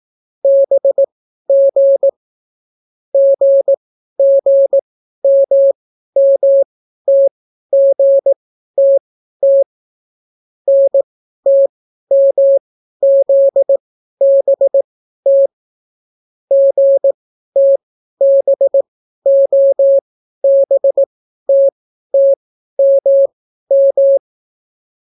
Morse code, heard as a keyed tone, about 18 wpm, spaced as for 12 wpm, 550 Hz. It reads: BG GGMMTGTT NTMZBT GTBOBTTMM